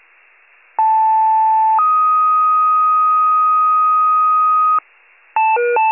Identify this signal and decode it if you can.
Two-tone page, heard as a one-second tone, then a three-second tone, about 880 then 1250 Hz. It is heard from 0.8 to 4.8 s.